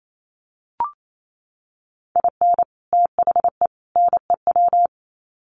Morse code, heard as a keyed tone, about 28 wpm, 700 Hz. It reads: IN THE NEW
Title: Sets of 3 Words 28wpm